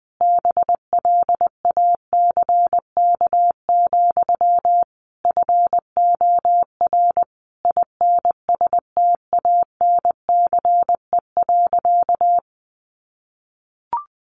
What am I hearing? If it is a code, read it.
Morse code, heard as a keyed tone, about 19 wpm, 700 Hz. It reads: BLACK, FOR INSTANCE.
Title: Sentences from Top 800 Words 20wpm